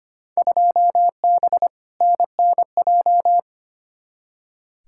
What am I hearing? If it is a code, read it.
Morse code, heard as a keyed tone, about 25 wpm, 700 Hz. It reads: 2B NNJ
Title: Field Day Exchanges Encore - 25wpm